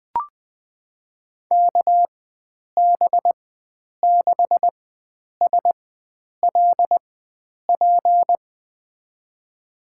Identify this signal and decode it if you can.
Morse code, heard as a keyed tone, about 21 wpm, spaced as for 9 wpm, 700 Hz. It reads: KB6SLP
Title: USA Call Signs - 6 Characters - 4x Character Spacing - 20wpm